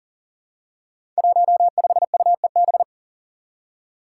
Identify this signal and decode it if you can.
Morse code, heard as a keyed tone, about 40 wpm, 700 Hz. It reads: 15UEB
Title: Five Challenging Letter-Number Combinations 40wpm